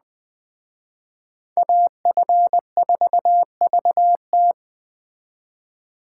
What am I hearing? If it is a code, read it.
Morse code, heard as a keyed tone, about 20 wpm, 700 Hz. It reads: AF4VT
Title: Speed Racing - USA 2x2 Call Signs 17wpm